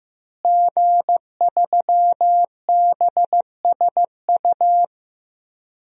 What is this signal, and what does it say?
Morse code, heard as a keyed tone, about 15 wpm, 700 Hz. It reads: G3BSU
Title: DXCC-100 Call Signs - 5 Characters - Repeated 3x - 15wpm